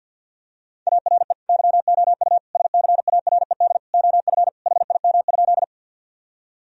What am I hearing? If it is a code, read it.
Morse code, heard as a keyed tone, about 50 wpm, 700 Hz. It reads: ARE YOU SCARED OF HIM?
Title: Sentences from Top 1000 Words - Review Part 4 - 50wpm